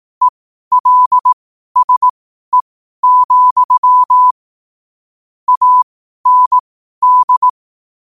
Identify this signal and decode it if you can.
Morse code, heard as a keyed tone, about 18 wpm, spaced as for 12 wpm, 1000 Hz. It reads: ELSE, AND
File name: qotdfiles / 12wpm